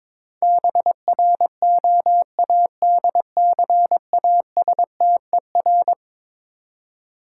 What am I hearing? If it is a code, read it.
Morse code, heard as a keyed tone, about 22 wpm, 700 Hz. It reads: BROADCASTER
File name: practice / 22wpm / Eleven Letter Words - 22wpm